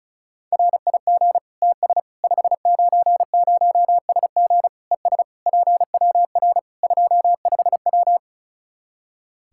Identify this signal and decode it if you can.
Morse code, heard as a keyed tone, about 34 wpm, 700 Hz. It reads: RIG TS 590SG ES PWR 25W
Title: K1USN SST QSO Practice 35wpm